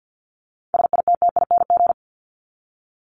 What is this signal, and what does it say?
Morse code, heard as a keyed tone, about 49 wpm, 700 Hz. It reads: SITTING
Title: ING Suffix - Top 100 Words with Base Words 50wpm